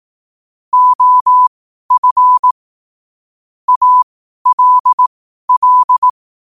Morse code, heard as a keyed tone, about 18 wpm, spaced as for 12 wpm, 1000 Hz. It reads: OF ALL
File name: qotdfiles / 12wpm